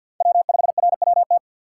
Morse code, heard as a keyed tone, about 48 wpm, 700 Hz. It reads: W5RWT
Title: USA Call Signs - 5 Characters - 4x Character Spacing - 50wpm